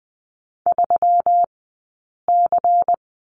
Morse code, heard as a keyed tone, about 20 wpm, spaced as for 8 wpm, 700 Hz. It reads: 3C
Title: 151 Lesson 25d - Letter p - Two Characters 20wpm